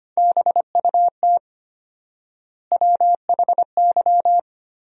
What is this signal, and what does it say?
Morse code, heard as a keyed tone, about 25 wpm, 700 Hz. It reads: BUT WHY
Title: 4x Word-Spacing - Sentences from Top 200 Words - 25wpm